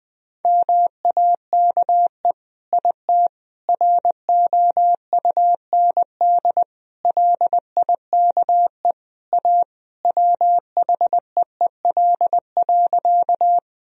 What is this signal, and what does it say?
Morse code, heard as a keyed tone, about 21 wpm, 700 Hz. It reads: MAKE IT ROUND LIKE A WHEEL.
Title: Sentences from Top 1000 Words - Review Part 3 - 20wpm